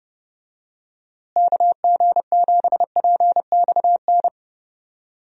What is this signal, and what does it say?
Morse code, heard as a keyed tone, about 31 wpm, 700 Hz. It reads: KG7PXN
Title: USA Call Signs - 6 Characters - Repeated 2x - 30wpm